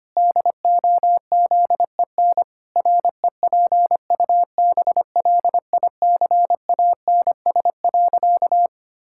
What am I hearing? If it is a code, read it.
Morse code, heard as a keyed tone, about 25 wpm, 700 Hz. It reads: DOZEN REPUBLICANS.
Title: Pangram Challenge v2 - 25wpm